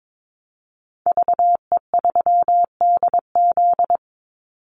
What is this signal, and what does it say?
Morse code, heard as a keyed tone, about 22 wpm, 700 Hz. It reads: VE3DZ
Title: Call Signs - 22wpm